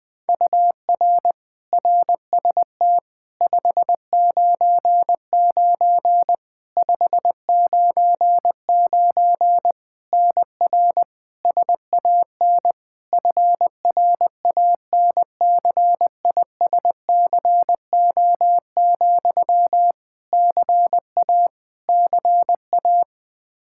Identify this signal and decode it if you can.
Morse code, heard as a keyed tone, about 20 wpm, 700 Hz. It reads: UR RST 599 599 NR SAN FRANCISCO, CA CA
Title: Practice QSO - Exchange 1 - Beginning - 20wpm